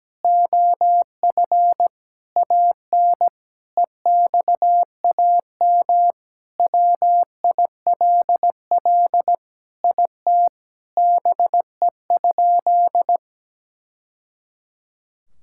Morse code, heard as a keyed tone, about 17 wpm, 700 Hz. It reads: OF AN EXAM WILL IT BE?
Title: Sentences from Top 1000 Words - Review Part 5 - 17wpm